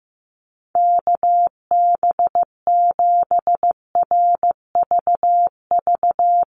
Morse code, heard as a keyed tone, about 15 wpm, 700 Hz. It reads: KB7RVV
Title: USA Call Signs 15wpm